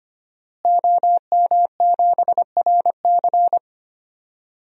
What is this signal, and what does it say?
Morse code, heard as a keyed tone, about 25 wpm, 700 Hz. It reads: OM7RC
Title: DXCC-100 Call Signs - 5 Characters - Repeated 2x - 25wpm